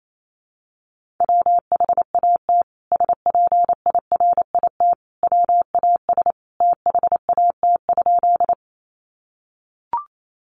Morse code, heard as a keyed tone, about 28 wpm, 700 Hz. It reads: WHAT SPIRIT WAS THAT?
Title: Sentences from Top 1400 Words 28wpm